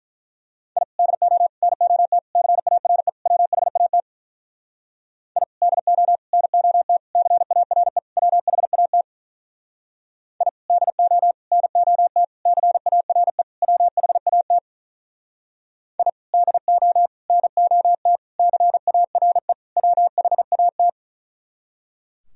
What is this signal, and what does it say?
Morse code, heard as a keyed tone, about 43 wpm, 700 Hz. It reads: I DO NOT CARE WHAT I DO NOT CARE WHAT I DO NOT CARE WHAT I DO NOT CARE WHAT
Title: Speed-Racing - Sets of 5 Words 35wpm